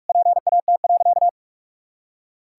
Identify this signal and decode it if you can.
Morse code, heard as a keyed tone, about 46 wpm, 700 Hz. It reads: PAT.
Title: Pangram Challenge v2 - 45wpm